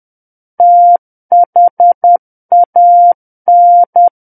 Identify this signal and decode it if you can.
Morse code, heard as a keyed tone, about 10 wpm, 700 Hz. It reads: THAN
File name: Book One / 01-Chapter 1 - 10 WPM